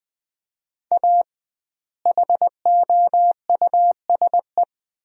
Morse code, heard as a keyed tone, about 20 wpm, 700 Hz. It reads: A HOUSE
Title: 2x Word-Spacing - Sentences from Top 400 Words - 20wpm